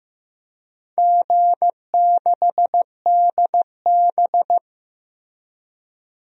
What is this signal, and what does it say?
Morse code, heard as a keyed tone, about 15 wpm, 700 Hz. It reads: G6DB
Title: DXCC-100 Call Signs - 4 Characters - Repeated 2x - 15wpm